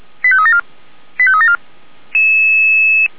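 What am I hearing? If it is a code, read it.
Five-tone selective call (ZVEI), heard as quick runs of five tones, tones of about 70 ms. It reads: 85374 85374
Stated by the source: ZVEI five-tone selective call 89246 received by SDR.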